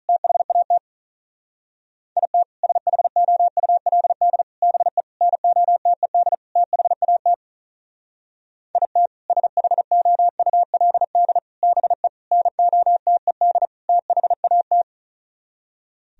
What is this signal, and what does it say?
Morse code, heard as a keyed tone, about 42 wpm, 700 Hz. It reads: THAT IT SHOULD BE NOTED THAT IT SHOULD BE NOTED THAT
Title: Speed-Racing - Sets of 5 Words 35wpm